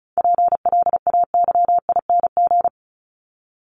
Morse code, heard as a keyed tone, about 34 wpm, 700 Hz. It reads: PLAYING
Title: ING Suffix - Top 100 Words with Base Words 35wpm